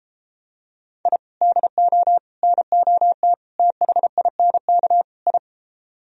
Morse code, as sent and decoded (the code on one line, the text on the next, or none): .. / -.. --- / -. --- - / - .... .. -. -.- / ..
I DO NOT THINK I